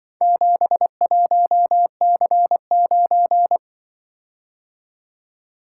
Morse code, heard as a keyed tone, about 25 wpm, 700 Hz. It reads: 71C9
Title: Speed-Racing - Four Letter-Number Combinations - 20wpm